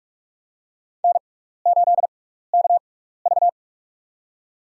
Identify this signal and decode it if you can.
Morse code, heard as a keyed tone, about 46 wpm, spaced as for 16 wpm, 700 Hz. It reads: N8KV
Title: USA Call Signs - 4 Characters - 6x Character Spacing - 45wpm